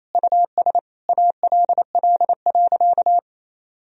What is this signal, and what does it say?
Morse code, heard as a keyed tone, about 28 wpm, 700 Hz. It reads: US ALL.
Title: Sentences from Top 1000 Words - Review Part 1 - 28wpm